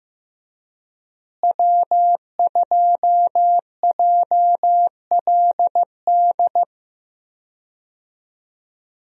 Morse code, heard as a keyed tone, about 15 wpm, 700 Hz. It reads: W2JLD
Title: USA Call Signs - 5 Characters - Repeated 2x - 15wpm